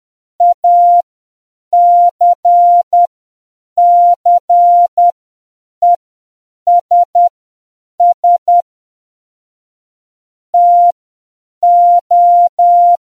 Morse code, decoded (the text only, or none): ACCESS TO